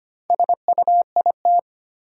Morse code, heard as a keyed tone, about 25 wpm, 700 Hz. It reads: SUIT
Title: Sentences from Top 1600 Words - New Word Drill - 25wpm